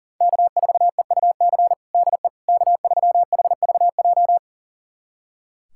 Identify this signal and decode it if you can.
Morse code, heard as a keyed tone, about 37 wpm, 700 Hz. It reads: K4EUC DE K3HVJ
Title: USA Callsign DE Callsign - 5 Characters - 40wpm